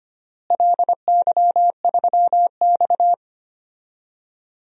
Morse code, heard as a keyed tone, about 25 wpm, 700 Hz. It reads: LY3X
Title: DXCC-100 Call Signs - 4 Characters - Repeated 3x - 25wpm